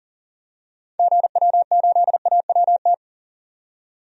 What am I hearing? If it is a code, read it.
Morse code, heard as a keyed tone, about 39 wpm, 700 Hz. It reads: GW8AWT
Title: DXCC-100 Call Signs - 6 Characters - 40wpm